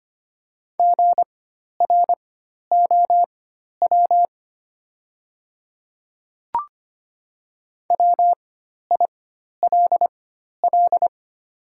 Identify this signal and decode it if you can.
Morse code, heard as a keyed tone, about 25 wpm, spaced as for 12 wpm, 700 Hz. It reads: GROW WILL
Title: 4x Character Spacing - Top 300 Words - 25wpm